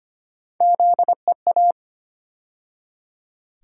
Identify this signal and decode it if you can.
Morse code, heard as a keyed tone, about 25 wpm, 700 Hz. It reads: ZEA
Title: USA Call Sign Postfixes - 3 Letters 25wpm